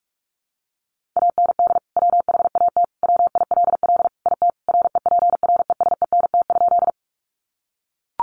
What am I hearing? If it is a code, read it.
Morse code, heard as a keyed tone, about 44 wpm, 700 Hz. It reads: AND WHAT WILL IT REPRESENT?
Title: Sentences from Top 1200 Words 45wpm